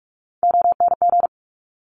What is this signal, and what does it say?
Morse code, heard as a keyed tone, about 46 wpm, 690 Hz. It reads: ONG
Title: Top Three Letter Combinations in Words 45wpm